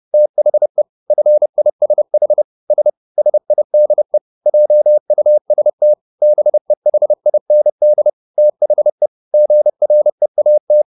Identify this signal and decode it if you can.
Morse code, heard as a keyed tone, about 30 wpm, 600 Hz. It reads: THE FISH S SIDE JUST BEHIND THE GREAT